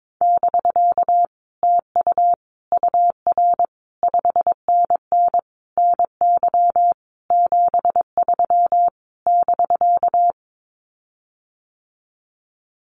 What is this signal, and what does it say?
Morse code, heard as a keyed tone, about 22 wpm, 700 Hz. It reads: <BK> TU UR 5NN NY 73 <BK>
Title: Common QSO Phrases Encore 22wpm